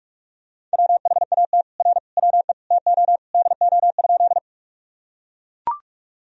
Morse code, heard as a keyed tone, about 42 wpm, 700 Hz. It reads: WHAT R WE TO DO?
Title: Sentences from Top 100 Words and Abbreviations 45wpm